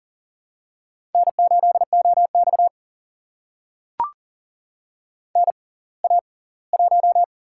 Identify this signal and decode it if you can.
Morse code, heard as a keyed tone, about 40 wpm, spaced as for 23 wpm, 700 Hz. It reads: N8OX NA1
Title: USA Call Signs - 4 Characters - 6x Character Spacing - 40wpm